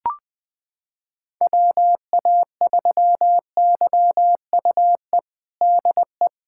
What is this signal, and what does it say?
Morse code, heard as a keyed tone, about 20 wpm, 700 Hz. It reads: WA3YUE DE
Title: USA Callsign DE Callsign 20wpm